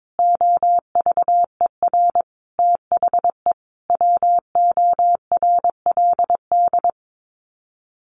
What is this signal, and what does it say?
Morse code, heard as a keyed tone, about 22 wpm, 700 Hz. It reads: OVER THE WORLD
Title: Sets of 5 Words 22wpm